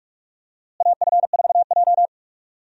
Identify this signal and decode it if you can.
Morse code, heard as a keyed tone, about 47 wpm, 700 Hz. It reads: AF4J
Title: USA Call Signs - 4 Characters - Repeated 2x - 45wpm